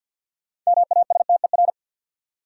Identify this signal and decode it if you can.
Morse code, heard as a keyed tone, about 53 wpm, 700 Hz. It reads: MASTER
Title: Sentences from Top 1300 Words - New Word Drill Alt - 50wpm